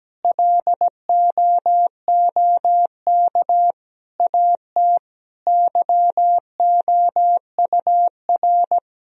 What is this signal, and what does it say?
Morse code, heard as a keyed tone, about 17 wpm, 700 Hz. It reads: LOOK AT YOUR
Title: Sentences from Top 1000 Words - Review Part 1 - 17wpm